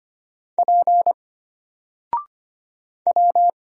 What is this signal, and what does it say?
Morse code, heard as a keyed tone, about 25 wpm, 700 Hz. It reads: P W
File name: Single-Letter-Number-25wpm